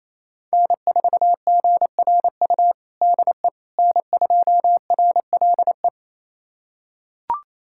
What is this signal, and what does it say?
Morse code, heard as a keyed tone, about 28 wpm, 700 Hz. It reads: N4GRU DE N2RLE
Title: USA Callsign DE Callsign - 5 Characters - 28wpm